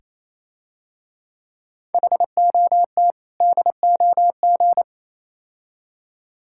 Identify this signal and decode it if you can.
Morse code, heard as a keyed tone, about 27 wpm, 700 Hz. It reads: HOT DOG
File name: Easy Words - Speed-Racing with Hints - 22wpm